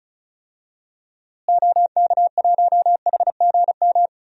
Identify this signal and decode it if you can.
Morse code, heard as a keyed tone, about 36 wpm, 700 Hz. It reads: OK1HGM
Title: DXCC-100 Call Signs - 6 Characters - Repeated 2x - 35wpm